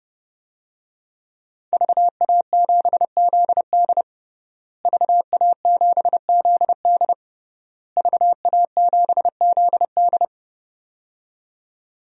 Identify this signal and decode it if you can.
Morse code, heard as a keyed tone, about 30 wpm, 700 Hz. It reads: VA7ZD VA7ZD VA7ZD
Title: DXCC-100 Call Signs - 5 Characters - Repeated 3x - 30wpm